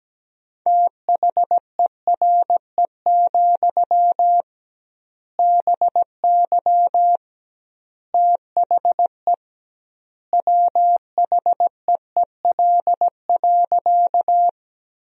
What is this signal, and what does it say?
Morse code, heard as a keyed tone, about 17 wpm, 700 Hz. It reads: THERE, BY THE WHEEL.
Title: 2x Word-Spacing - Sentences from Top 500 Words - 17wpm